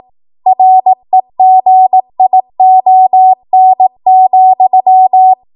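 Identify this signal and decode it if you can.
Morse code, heard as a keyed tone, about 18 wpm, 750 Hz. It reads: REGION,
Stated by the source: ARRL 18 wpm Morse code practice archive transmission.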